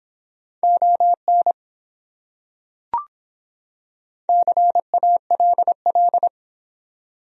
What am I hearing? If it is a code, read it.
Morse code, heard as a keyed tone, about 25 wpm, 700 Hz. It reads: ON CALL